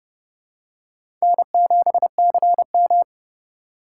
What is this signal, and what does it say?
Morse code, heard as a keyed tone, about 30 wpm, 700 Hz. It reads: N7CM
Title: USA Call Signs - 4 Characters - 3x Character Spacing - 30wpm